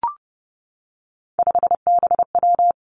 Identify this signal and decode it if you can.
Morse code, heard as a keyed tone, about 30 wpm, 700 Hz. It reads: 5BW
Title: Three Challenging Letter-Number Combinations 30wpm